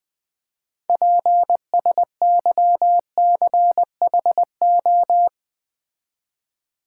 Morse code, heard as a keyed tone, about 20 wpm, 700 Hz. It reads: PSYCHO
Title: Build A Word - 6 Letters At A Time - 12-Letter Words - 20wpm